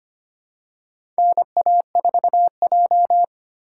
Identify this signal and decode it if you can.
Morse code, heard as a keyed tone, about 25 wpm, 700 Hz. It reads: NA4J
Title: USA Call Signs - 3x Character Spacing - 25wpm